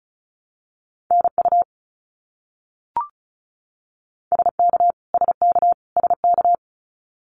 Morse code, heard as a keyed tone, about 34 wpm, 700 Hz. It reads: NU SK SK SK
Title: Canadian Provinces and Territories - Repeated 3x - 35wpm